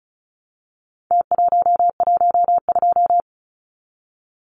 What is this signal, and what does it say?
Morse code, heard as a keyed tone, about 34 wpm, 700 Hz. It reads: T112
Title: Numbers - Contest Style - 4-Digit 35wpm